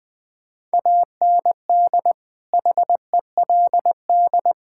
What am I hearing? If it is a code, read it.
Morse code, heard as a keyed tone, about 20 wpm, 700 Hz. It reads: AND HELD
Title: Sets of 9 Words - 20wpm